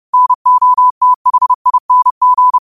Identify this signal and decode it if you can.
Morse code, heard as a keyed tone, about 30 wpm, 1000 Hz. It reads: NOTHING